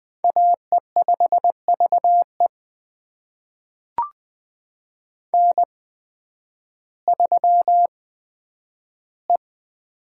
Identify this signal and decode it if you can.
Morse code, heard as a keyed tone, about 20 wpm, spaced as for 11 wpm, 700 Hz. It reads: AE5VE N3E
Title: USA Call Signs - 5 Characters - 8x Character Spacing - 20wpm